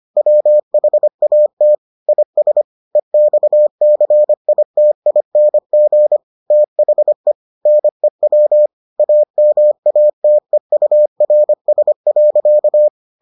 Morse code, heard as a keyed tone, about 25 wpm, 600 Hz. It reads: WHAT IS EXCITING THE NEW AMATEURS.